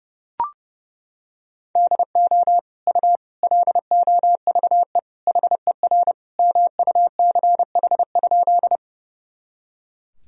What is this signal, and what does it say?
Morse code, heard as a keyed tone, about 30 wpm, 700 Hz. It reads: DO U LOVE HER MUCH?
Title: Sentences from Top 400 Words and Abbreviations 30wpm